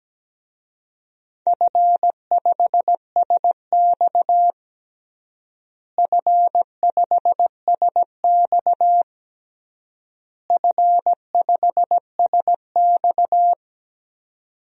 Morse code, heard as a keyed tone, about 17 wpm, 700 Hz. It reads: F5SX F5SX F5SX
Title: DXCC-100 Call Signs - 4 Characters - Repeated 3x - 17wpm